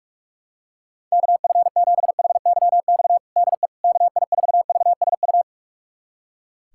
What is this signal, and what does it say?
Morse code, heard as a keyed tone, about 41 wpm, 700 Hz. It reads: KF7HYX DE KI4VSU
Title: USA Callsign DE Callsign - 6 Characters - 45wpm